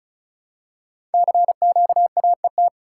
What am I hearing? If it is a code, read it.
Morse code, heard as a keyed tone, about 36 wpm, 690 Hz. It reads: CQAET
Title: Code Groups - 5 Letter:Numbers - 35wpm